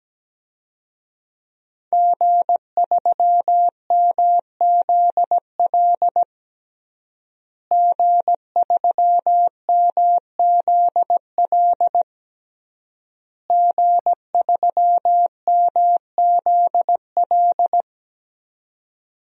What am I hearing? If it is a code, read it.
Morse code, heard as a keyed tone, about 17 wpm, 700 Hz. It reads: G3MZL G3MZL G3MZL
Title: DXCC-100 Call Signs - 5 Characters - Repeated 3x - 17wpm